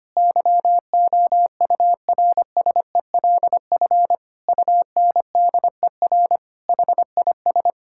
Morse code, heard as a keyed tone, about 25 wpm, 700 Hz. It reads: YOURSELF UNDER HIS
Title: Sentences from Top 1600 Words 25wpm